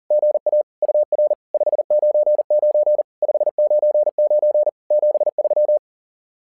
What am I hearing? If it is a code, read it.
Morse code, heard as a keyed tone, about 40 wpm, 600 Hz. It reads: GA UR 599 599 73
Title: SOTA QSO Practice 40wpm